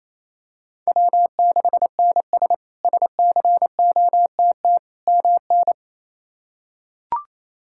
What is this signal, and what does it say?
Morse code, heard as a keyed tone, about 28 wpm, 700 Hz. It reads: W6NS SCOTT MN
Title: K1USN SST QSO Practice - 28wpm